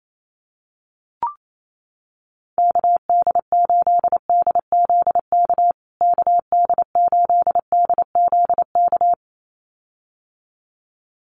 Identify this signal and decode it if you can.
Morse code, heard as a keyed tone, about 28 wpm, 700 Hz. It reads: KD8DZK KD8DZK
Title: USA Call Signs Repeated 2x 28wpm